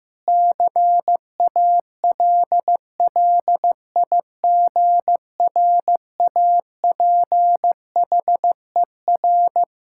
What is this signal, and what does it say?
Morse code, heard as a keyed tone, about 15 wpm, 700 Hz. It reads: CALLIGRAPHER
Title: Pangram Challenge v2 - 15wpm